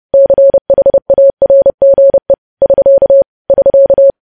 Morse code, heard as a keyed tone, about 30 wpm, 550 Hz. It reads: CHARGE <SK> <SK>
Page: News Headlines in Morse Code at 30 WPM